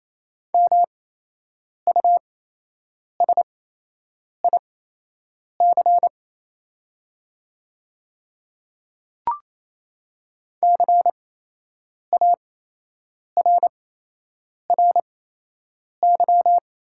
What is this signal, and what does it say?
Morse code, heard as a keyed tone, about 28 wpm, spaced as for 8 wpm, 700 Hz. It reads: MUSIC CARRY
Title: 8x Character Spacing - Top 500 Words - 28wpm